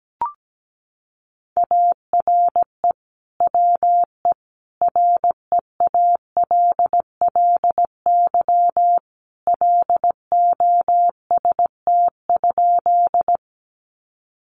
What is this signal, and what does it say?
Morse code, heard as a keyed tone, about 17 wpm, 700 Hz. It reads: ARE WE REALLY LOST?
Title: Sentences from Top 1400 Words 17wpm